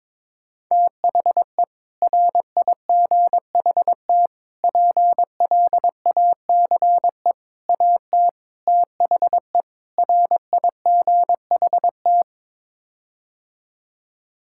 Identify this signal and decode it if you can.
Morse code, heard as a keyed tone, about 21 wpm, 700 Hz. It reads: THE RIGHT PLACE AT THE RIGHT
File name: practice / 22wpm / Sets of 6 Words - 22wpm